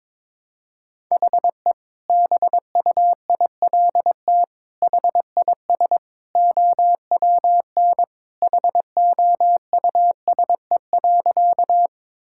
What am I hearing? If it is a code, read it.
Morse code, heard as a keyed tone, about 22 wpm, 700 Hz. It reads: HE BUILT HIS OWN HOUSE.
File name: practice / 22wpm / Sentences with 2-Word Phrases - 22wpm